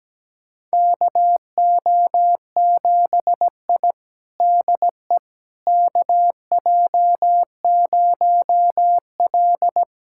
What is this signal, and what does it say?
Morse code, heard as a keyed tone, about 17 wpm, 700 Hz. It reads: KO7I DE KJ0L
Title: USA Callsign DE Callsign - 4 Characters - 17wpm